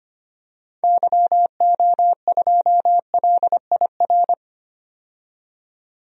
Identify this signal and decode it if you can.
Morse code, heard as a keyed tone, about 25 wpm, 700 Hz. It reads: YO2LIR
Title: DXCC-100 Call Signs - 6 Characters - 25wpm